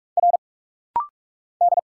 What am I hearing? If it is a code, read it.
Morse code, heard as a keyed tone, about 46 wpm, 700 Hz. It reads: R D
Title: Single Letters - Rapid-Fire 45wpm